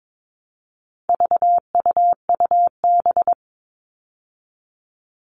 Morse code, heard as a keyed tone, about 22 wpm, 700 Hz. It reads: VUUB
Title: All Cut Numbers - 4-Digit 22wpm